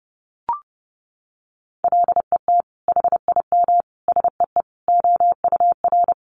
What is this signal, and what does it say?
Morse code, heard as a keyed tone, about 30 wpm, 700 Hz. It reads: LET HIM SEE OUR
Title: Sentences from Top 1400 Words 30wpm